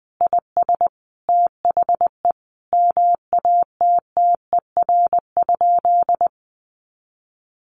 Morse code, heard as a keyed tone, about 20 wpm, 700 Hz. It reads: IS THE MATTER?
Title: Sentences from Top 700 Words 20wpm